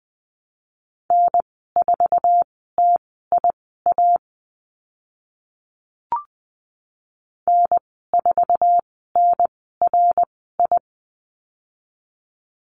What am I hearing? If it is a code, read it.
Morse code, heard as a keyed tone, about 20 wpm, spaced as for 14 wpm, 700 Hz. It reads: N4TIA N4NRI
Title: 069 Lesson 12g - Letter d - Call Signs 20wpm